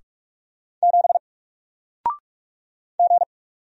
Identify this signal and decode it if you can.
Morse code, heard as a keyed tone, about 44 wpm, 700 Hz. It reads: 7 G
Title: Single Letter-Number Alt 45wpm